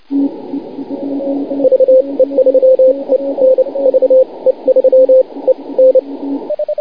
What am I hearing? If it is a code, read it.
Morse code, heard as a keyed tone, about 29 wpm, spaced as for 22 wpm, 520 Hz, with radio noise. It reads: VE3EN VE3EN
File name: wp / ve3en 6m cw 2149z june 11th 2010